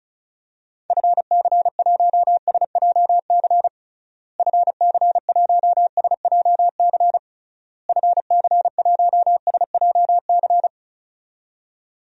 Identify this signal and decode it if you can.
Morse code, heard as a keyed tone, about 36 wpm, 700 Hz. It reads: FC1SJC FC1SJC FC1SJC
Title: DXCC-100 Call Signs - 6 Characters - Repeated 3x - 35wpm